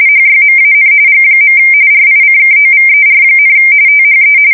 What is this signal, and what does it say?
RTTY, about 45.45 baud. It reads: HYMAN G. RICKOVER, THE
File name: fsk170 / usnavy5